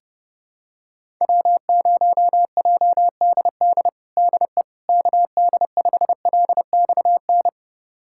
Morse code, heard as a keyed tone, about 30 wpm, 700 Hz. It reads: W0JDD DE KD5LXN
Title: USA Callsign DE Callsign 30wpm